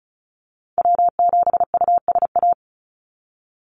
Morse code, heard as a keyed tone, about 34 wpm, 700 Hz. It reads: W7USA
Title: NA SOTA Activator Callsigns 35wpm